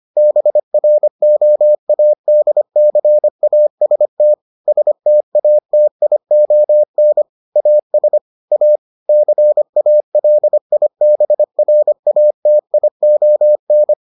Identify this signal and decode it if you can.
Morse code, heard as a keyed tone, about 25 wpm, 600 Hz. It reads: BROADCAST STATION AS A CALIBRATION